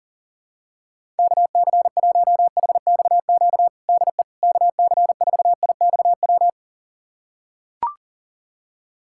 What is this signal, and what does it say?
Morse code, heard as a keyed tone, about 40 wpm, 700 Hz. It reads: KC1HXQ DE KC4IXW
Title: USA Callsign DE Callsign - 6 Characters - 40wpm